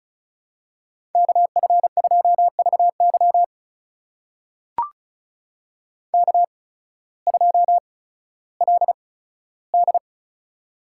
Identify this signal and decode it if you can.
Morse code, heard as a keyed tone, about 36 wpm, spaced as for 16 wpm, 700 Hz. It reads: KF2VY K2LD
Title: USA Call Signs - 8x Character Spacing - 35wpm